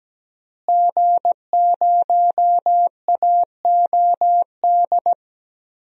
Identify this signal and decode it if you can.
Morse code, heard as a keyed tone, about 17 wpm, 700 Hz. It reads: G0AOD
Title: DXCC-100 Call Signs - 5 Characters - Repeated 2x - 17wpm